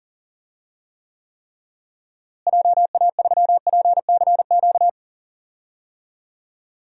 Morse code, heard as a keyed tone, about 39 wpm, 700 Hz. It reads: JA3PCQ
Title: DXCC-100 Call Signs - 6 Characters - Repeated 3x - 40wpm